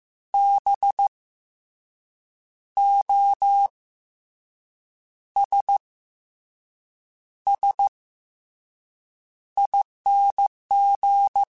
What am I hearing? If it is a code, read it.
Morse code, heard as a keyed tone, about 15 wpm, 800 Hz. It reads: B O S S ING
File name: Special INGs / Special INGs-mp3s / SlowToFaster ing @13